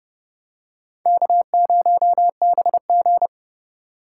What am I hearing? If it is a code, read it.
Morse code, heard as a keyed tone, about 30 wpm, 700 Hz. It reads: K0BG